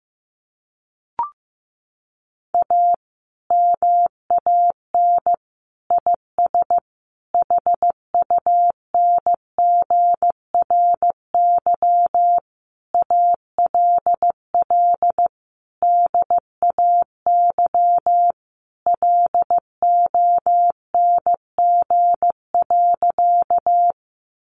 Morse code, heard as a keyed tone, about 15 wpm, 700 Hz. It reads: A MAN IS HUNGRY ALL DAY LONG.
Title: Sentences from Top 1400 Words 15wpm